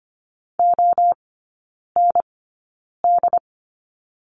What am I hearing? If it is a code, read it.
Morse code, heard as a keyed tone, about 25 wpm, spaced as for 9 wpm, 700 Hz. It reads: OND
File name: Lesson 17e - Letter u - Three Characters 25wpm